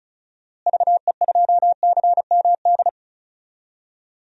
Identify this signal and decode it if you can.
Morse code, heard as a keyed tone, about 36 wpm, 700 Hz. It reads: VE2CMD
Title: DXCC-100 Call Signs - 6 Characters - Repeated 3x - 35wpm